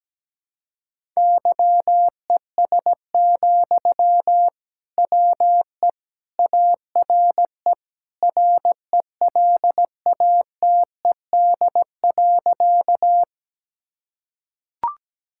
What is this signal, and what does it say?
Morse code, heard as a keyed tone, about 17 wpm, 700 Hz. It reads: YES, WE ARE RELATED.
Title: Sentences from Top 1000 Words - Review Part 5 - 17wpm